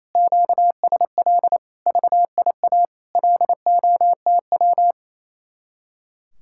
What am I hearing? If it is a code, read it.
Morse code, heard as a keyed tone, about 34 wpm, 700 Hz. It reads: QSL VIA LOTW
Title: Speed-Racing Common QSO Phrases - 28wpm